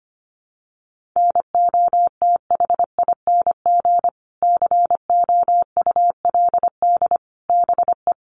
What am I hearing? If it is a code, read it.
Morse code, heard as a keyed tone, about 25 wpm, 700 Hz. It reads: NOTHING COULD BE
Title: Sentences from Top 900 Words 25wpm